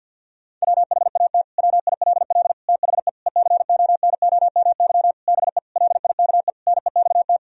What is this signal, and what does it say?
Morse code, heard as a keyed tone, about 49 wpm, 700 Hz. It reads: WHAT WILL THE ECONOMY BE LIKE NEXT
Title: Sentences from Top 1000 Words - Review Part 3 - 50wpm